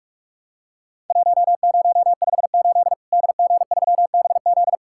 Morse code, heard as a keyed tone, about 46 wpm, 700 Hz. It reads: 1058 DG3BZ
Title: Worked All Europe DX Contest - QTC Exchange - 45wpm